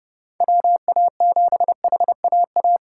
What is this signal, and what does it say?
Morse code, heard as a keyed tone, about 30 wpm, 700 Hz. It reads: WA7HAA